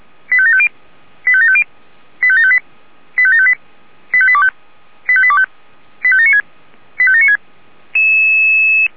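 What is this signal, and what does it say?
Five-tone selective call (ZVEI), heard as quick runs of five tones, tones of about 70 ms. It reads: 85750 85750 85758 85758 85825 85825 85796 85796
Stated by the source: ZVEI five-tone call received on an SDR.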